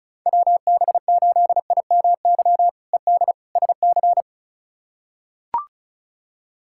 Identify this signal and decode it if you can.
Morse code, heard as a keyed tone, about 36 wpm, 700 Hz. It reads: WB8IMY ED SC
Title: K1USN SST QSO Practice - 35wpm